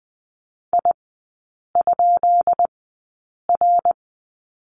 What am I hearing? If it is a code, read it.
Morse code, heard as a keyed tone, about 20 wpm, spaced as for 8 wpm, 700 Hz. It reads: I?R
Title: 133 Lesson 22e - Question Mark - Three Characters 20wpm